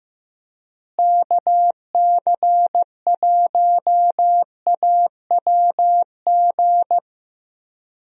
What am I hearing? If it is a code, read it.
Morse code, heard as a keyed tone, about 15 wpm, 700 Hz. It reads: KC1AWG
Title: USA Call Signs - 6 Characters - Repeated 3x - 15wpm